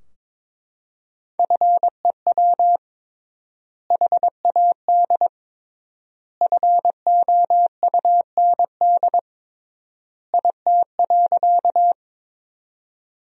Morse code, heard as a keyed tone, about 22 wpm, 700 Hz. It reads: FEW HAD FOUND IT.
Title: 3x Word-Spacing - Sentences from Top 300 Words - 22wpm